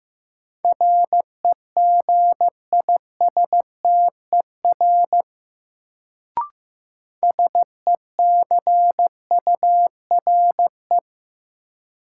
Morse code, heard as a keyed tone, about 15 wpm, 700 Hz. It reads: REGISTER SECURE
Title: Sentences from Top 1200 Words - New Word Drill - 15wpm